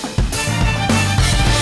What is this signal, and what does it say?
There is no code.